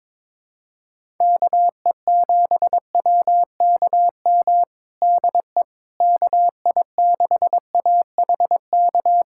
Morse code, heard as a keyed tone, about 22 wpm, 700 Hz. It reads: KE7WKM DE KI6AHK
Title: USA Callsign DE Callsign - 6 Characters - 22wpm